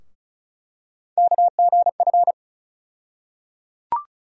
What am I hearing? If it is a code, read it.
Morse code, heard as a keyed tone, about 36 wpm, 700 Hz. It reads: KGF
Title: USA Call Sign Postfixes - 3 Letters 35wpm